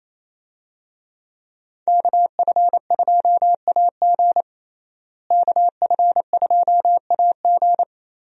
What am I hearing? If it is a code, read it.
Morse code, heard as a keyed tone, about 28 wpm, 700 Hz. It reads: KF2AG KF2AG
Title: USA Call Signs - 5 Characters - Repeated 3x - 28wpm